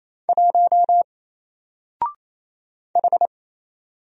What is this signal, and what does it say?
Morse code, heard as a keyed tone, about 27 wpm, 700 Hz. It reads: 1 H
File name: Single-Letter-Number-28wpm